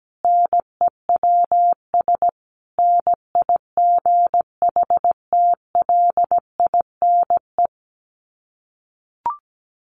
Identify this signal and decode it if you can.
Morse code, heard as a keyed tone, about 17 wpm, 700 Hz. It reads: NEWS NIGHTLINE
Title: Sets of 5 Words 17wpm